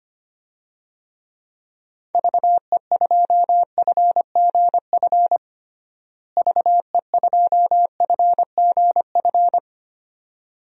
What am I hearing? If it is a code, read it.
Morse code, heard as a keyed tone, about 25 wpm, 700 Hz. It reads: VE2FGF VE2FGF
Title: DXCC-100 Call Signs - 6 Characters - Repeated 2x - 25wpm